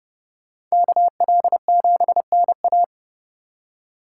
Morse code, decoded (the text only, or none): KL7NA